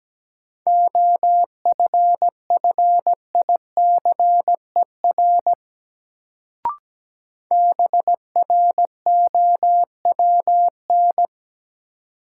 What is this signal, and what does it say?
Morse code, heard as a keyed tone, about 17 wpm, 700 Hz. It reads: OFFICER BROWN
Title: Sentences from Top 1200 Words - New Word Drill - 17wpm